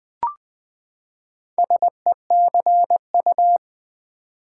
Morse code, heard as a keyed tone, about 20 wpm, 700 Hz. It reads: SECU
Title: Build A Word - 4 Letters At A Time - 8-Letter Words - 20wpm